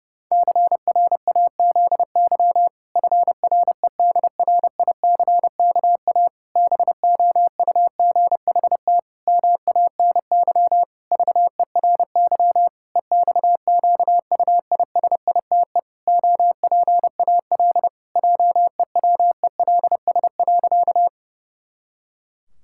Morse code, heard as a keyed tone, about 30 wpm, 700 Hz. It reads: CRAZY FREDRICKA BOUGHT MANY VERY EXQUISITE OPAL JEWELS.
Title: Pangram Challenge v2 - 30wpm